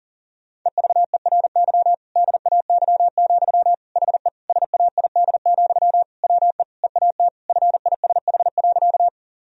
Morse code, heard as a keyed tone, about 40 wpm, 700 Hz. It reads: EVERY DAY, HE SAID, WE EAT FISH.
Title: Sentences from Top 1000 Words - Review Part 1 - 40wpm